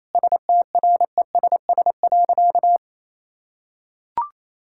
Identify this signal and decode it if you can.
Morse code, heard as a keyed tone, about 27 wpm, 700 Hz. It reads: STRESS.
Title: Sentences from Top 1000 Words - Review Part 6 - 28wpm